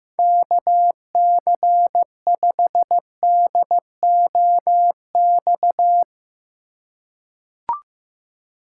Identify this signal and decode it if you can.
Morse code, heard as a keyed tone, about 15 wpm, 700 Hz. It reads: KC5DOX
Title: USA Call Signs - 6 Characters - 3x Character Spacing - 15wpm